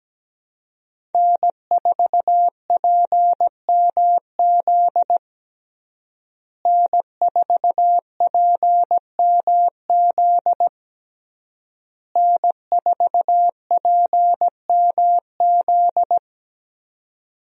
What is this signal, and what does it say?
Morse code, heard as a keyed tone, about 17 wpm, 700 Hz. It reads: N4PMZ N4PMZ N4PMZ
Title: USA Call Signs - 5 Characters - Repeated 3x - 17wpm